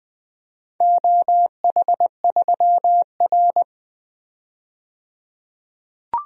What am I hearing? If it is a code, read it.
Morse code, heard as a keyed tone, about 20 wpm, 700 Hz. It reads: OH3R
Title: DXCC-100 Call Signs - 4 Characters - 20wpm